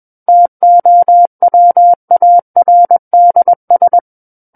Morse code, heard as a keyed tone, about 21 wpm, 700 Hz. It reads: TOWARDS
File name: Book One / 12-Chapter 12 - 21 WPM